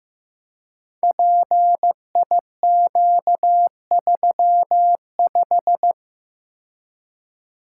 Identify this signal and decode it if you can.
Morse code, heard as a keyed tone, about 15 wpm, 700 Hz. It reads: PIQ35
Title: Code Groups - 5 Letter:Numbers - 15wpm